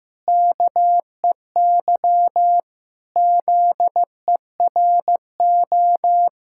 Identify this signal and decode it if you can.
Morse code, heard as a keyed tone, about 15 wpm, 700 Hz. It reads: KEY ZERO
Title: Pangram Challenge v2 - 15wpm